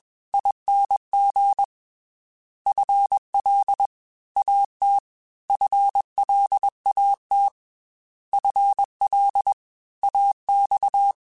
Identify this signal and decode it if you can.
Morse code, heard as a keyed tone, about 21 wpm, 800 Hz. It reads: ING FL AT FLAT FL AX
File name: FL / FL mp3s / Fast FL @20